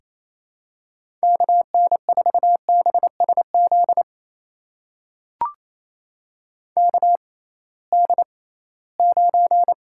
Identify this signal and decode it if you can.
Morse code, heard as a keyed tone, about 28 wpm, spaced as for 18 wpm, 700 Hz. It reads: KN4BSZ KD9
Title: USA Call Signs - 6 Characters - 6x Character Spacing - 28wpm